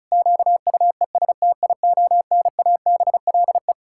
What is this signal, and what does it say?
Morse code, heard as a keyed tone, about 36 wpm, 700 Hz. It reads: QUESTIONABLE
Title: Pangram Challenge v2 - 35wpm